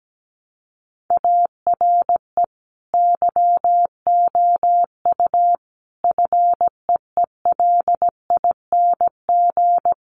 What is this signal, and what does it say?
Morse code, heard as a keyed tone, about 17 wpm, 700 Hz. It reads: ARE YOU FEELING
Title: Sentences from Top 1200 Words 17wpm